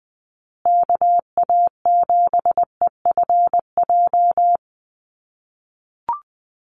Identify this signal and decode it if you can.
Morse code, heard as a keyed tone, about 20 wpm, 700 Hz. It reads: KA7EFJ
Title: USA Call Signs Repeated 2x 20wpm